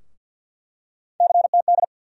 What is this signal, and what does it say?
Morse code, heard as a keyed tone, about 52 wpm, 700 Hz. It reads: CTD
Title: USA Call Sign Postfixes - 3 Letters 50wpm